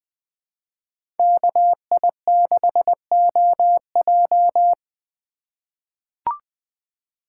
Morse code, heard as a keyed tone, about 21 wpm, 700 Hz. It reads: KI6OJ
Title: USA Call Signs - 5 Characters - 4x Character Spacing - 20wpm